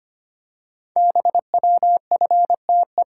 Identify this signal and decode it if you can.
Morse code, heard as a keyed tone, about 25 wpm, 700 Hz. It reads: BWFTE
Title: Code Groups - 5 Letters - 25wpm